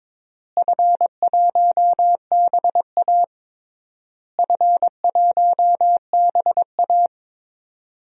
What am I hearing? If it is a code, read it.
Morse code, heard as a keyed tone, about 22 wpm, 700 Hz. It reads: F1BA F1BA
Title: DXCC-100 Call Signs - 4 Characters - Repeated 3x - 22wpm